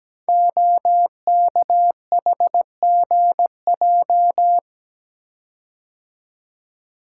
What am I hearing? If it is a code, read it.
Morse code, heard as a keyed tone, about 17 wpm, 700 Hz. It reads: OKHGJ
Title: Code Groups - 5 Letter:Numbers - 17wpm